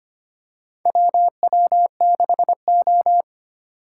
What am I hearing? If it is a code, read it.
Morse code, heard as a keyed tone, about 25 wpm, 700 Hz. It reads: WW6O
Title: USA Call Signs - 4 Characters - 8x Character Spacing - 25wpm